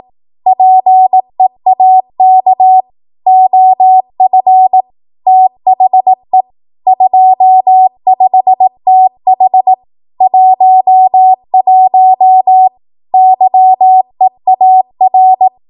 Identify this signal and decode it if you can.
Morse code, heard as a keyed tone, about 18 wpm, 750 Hz. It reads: PEAK OF THE 25TH 11 YEAR
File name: Morse / 18WPM